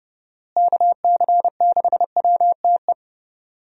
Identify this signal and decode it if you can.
Morse code, heard as a keyed tone, about 30 wpm, 700 Hz. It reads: KC6WTE
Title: USA Call Signs - 6 Characters - Repeated 2x - 30wpm